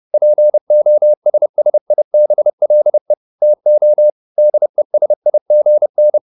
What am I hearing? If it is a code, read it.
Morse code, heard as a keyed tone, about 30 wpm, 600 Hz. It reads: POSSIBLE TO DESIGN